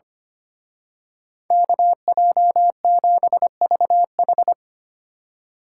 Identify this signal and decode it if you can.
Morse code, heard as a keyed tone, about 28 wpm, 700 Hz. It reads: KJ7VH
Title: Speed Racing - USA 2x2 Call Signs 25wpm